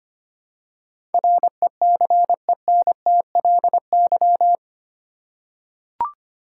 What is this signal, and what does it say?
Morse code, heard as a keyed tone, about 25 wpm, 700 Hz. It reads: RECENTLY
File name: 8 Letter Words - First 3 Letters Given - 25wpm